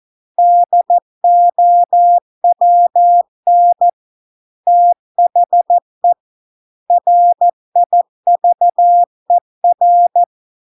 Morse code, heard as a keyed tone, about 14 wpm, 700 Hz. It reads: DOWN THE RIVER